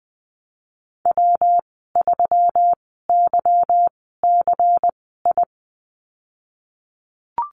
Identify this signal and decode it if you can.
Morse code, heard as a keyed tone, about 20 wpm, spaced as for 15 wpm, 700 Hz. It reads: W3YCI
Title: 173 Lesson 28g - Number 9 - Call Signs 20wpm